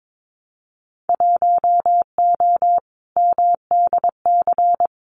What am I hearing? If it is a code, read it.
Morse code, heard as a keyed tone, about 22 wpm, 700 Hz. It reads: 1O MDC
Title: Winter Field Day Exchanges 22wpm